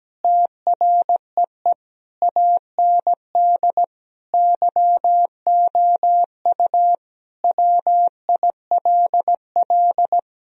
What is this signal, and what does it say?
Morse code, heard as a keyed tone, about 17 wpm, 700 Hz. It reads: TREE AND YOU WILL
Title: Sentences from Top 1000 Words - Review Part 1 - 17wpm